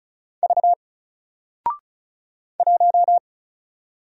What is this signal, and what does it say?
Morse code, heard as a keyed tone, about 35 wpm, 700 Hz. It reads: V 1
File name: Single-Letter-Number-35wpm